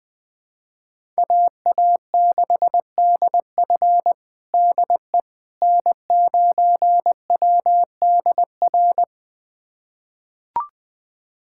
Morse code, heard as a keyed tone, about 20 wpm, 700 Hz. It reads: AA6DF DE N9WDR
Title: USA Callsign DE Callsign - 5 Characters - 20wpm